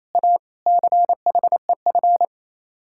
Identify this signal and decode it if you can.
Morse code, heard as a keyed tone, about 35 wpm, 700 Hz. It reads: A CHEF